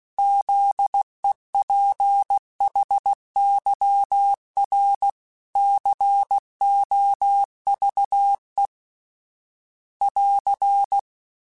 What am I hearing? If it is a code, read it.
Morse code, heard as a keyed tone, about 16 wpm, 800 Hz. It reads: ZEPHYR COVE <AR>
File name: fromSMs / FromSMs mp3s / Cities@15